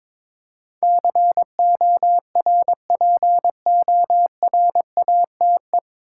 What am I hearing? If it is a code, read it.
Morse code, heard as a keyed tone, about 22 wpm, 700 Hz. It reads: CORPORATE
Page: Morse Code Ninja practice 22wpm